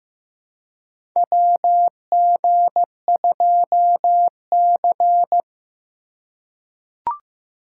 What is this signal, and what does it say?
Morse code, heard as a keyed tone, about 15 wpm, 700 Hz. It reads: WG2C
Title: USA Call Signs - 3x Character Spacing - 15wpm